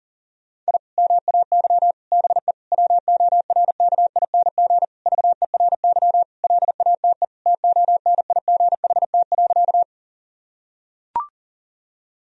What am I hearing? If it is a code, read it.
Morse code, heard as a keyed tone, about 37 wpm, 700 Hz. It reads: I MAY BE WORKING VERY LATE TONIGHT.
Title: Sentences from Top 1000 Words - Review Part 4 - 40wpm